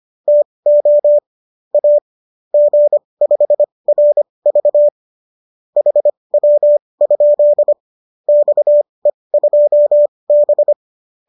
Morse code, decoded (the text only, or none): TO A G5RV HW? XE2B